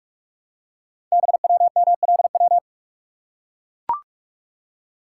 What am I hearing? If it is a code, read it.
Morse code, heard as a keyed tone, about 44 wpm, 700 Hz. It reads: BWMLW
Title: Code Groups - 5 Letters - 45wpm